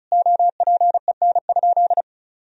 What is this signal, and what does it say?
Morse code, heard as a keyed tone, about 36 wpm, 700 Hz. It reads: OPEN?
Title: Sentences from Top 1000 Words - Review Part 1 - 35wpm